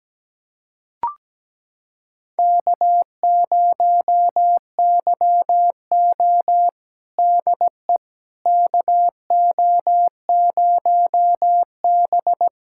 Morse code, heard as a keyed tone, about 17 wpm, 700 Hz. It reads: K0YO DE KO0B
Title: USA Callsign DE Callsign - 4 Characters - 17wpm